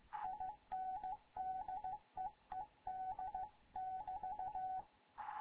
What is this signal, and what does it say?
Morse code, heard as a keyed tone, about 15 wpm, 750 Hz, with radio noise. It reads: INDEED <BT>
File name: Morze-HA2MN / hangp / 2 11 GYAKORLAT